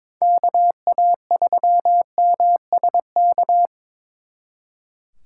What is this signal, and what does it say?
Morse code, heard as a keyed tone, about 25 wpm, 700 Hz. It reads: KA3MSK
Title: Speed Racing - USA 2x3 Call Signs 22wpm